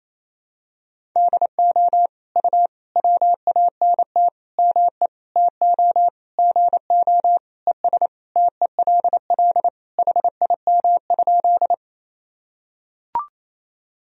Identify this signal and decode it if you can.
Morse code, heard as a keyed tone, about 27 wpm, 700 Hz. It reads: DO U WANT ME TO GO ES TELL HIM?
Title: Sentences from Top 200 Words and Abbreviations 28wpm